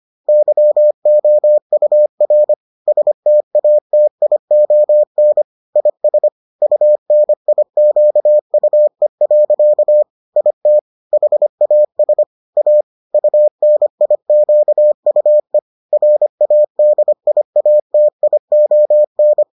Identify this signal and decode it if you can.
Morse code, heard as a keyed tone, about 25 wpm, 600 Hz. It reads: YOUR STATION IS UNIQUE. IT HAS A UNIQUE RADIATION